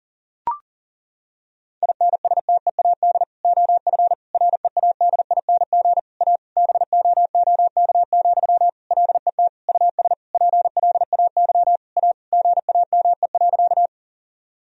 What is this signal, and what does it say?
Morse code, heard as a keyed tone, about 40 wpm, 700 Hz. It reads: INSTEAD OF READING A BOOK, LET US PLAY A GAME.
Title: Sentences from Top 1000 Words - Review Part 4 - 40wpm